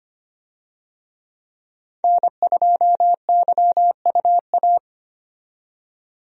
Morse code, heard as a keyed tone, about 25 wpm, 690 Hz. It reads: N2YUA
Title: USA Call Signs - 5 Characters - 25wpm